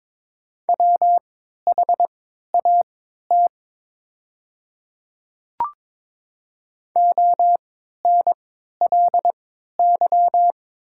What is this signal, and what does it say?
Morse code, heard as a keyed tone, about 21 wpm, spaced as for 12 wpm, 700 Hz. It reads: WHAT ONLY
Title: 3x Character Spacing - Top 200 Words - 22wpm